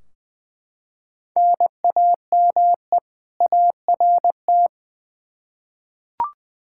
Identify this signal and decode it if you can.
Morse code, heard as a keyed tone, about 20 wpm, 700 Hz. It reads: NAME ART
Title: K1USN SST QSO Practice 20wpm